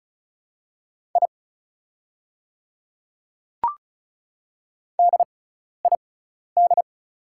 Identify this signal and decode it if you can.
Morse code, heard as a keyed tone, about 36 wpm, spaced as for 12 wpm, 690 Hz. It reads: I DID
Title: 6x Character Spacing - Top 100 Words - 35wpm